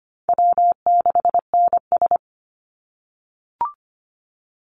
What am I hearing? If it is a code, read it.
Morse code, heard as a keyed tone, about 25 wpm, 700 Hz. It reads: W6NS
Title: Easy Call Signs - 25wpm